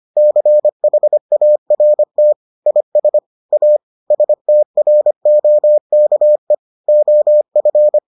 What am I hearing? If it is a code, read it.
Morse code, heard as a keyed tone, about 25 wpm, 600 Hz. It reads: CHART IS A STROKE OF